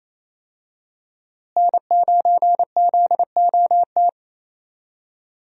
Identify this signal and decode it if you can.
Morse code, heard as a keyed tone, about 28 wpm, 700 Hz. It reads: N9ZOT
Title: USA Call Signs - 5 Characters - Repeated 2x - 28wpm